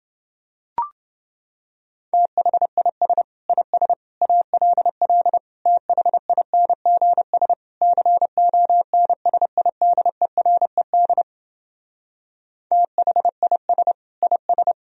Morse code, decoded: THIS IS ALL THINGS CONSIDERED THIS IS

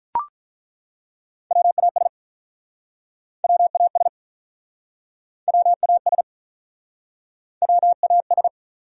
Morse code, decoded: WAS WAS WAS WAS

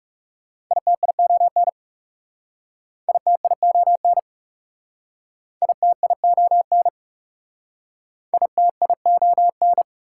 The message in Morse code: ITION ITION ITION ITION

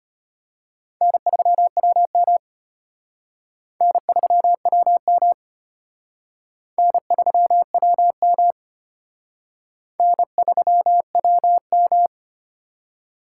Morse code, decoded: N3WM N3WM N3WM N3WM

